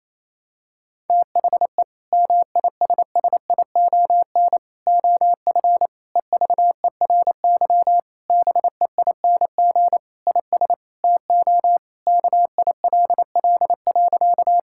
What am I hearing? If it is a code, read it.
Morse code, heard as a keyed tone, about 28 wpm, 700 Hz. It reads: THE MISSION OF EVERY BEING IS TO KILL.
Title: Sentences from Top 1600 Words 28wpm